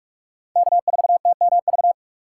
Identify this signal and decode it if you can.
Morse code, heard as a keyed tone, about 45 wpm, 700 Hz. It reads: K4TMV
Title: USA Call Signs - 5 Characters - 2x Character Spacing - 45wpm